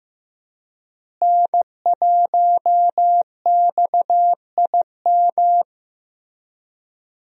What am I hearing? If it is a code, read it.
Morse code, heard as a keyed tone, about 16 wpm, 700 Hz. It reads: N1XIM
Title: USA Call Signs - 5 Characters - Repeated 2x - 15wpm